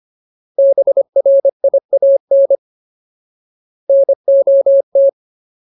Morse code, heard as a keyed tone, about 25 wpm, 550 Hz. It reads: BRIAN NOT
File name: Ultimate Word List / 25WPM Ultimate